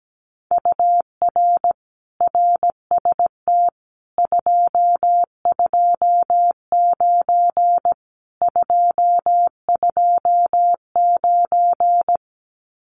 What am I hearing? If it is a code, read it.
Morse code, heard as a keyed tone, about 17 wpm, 700 Hz. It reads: UR RST 229 229
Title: Common QSO Phrases Encore 17wpm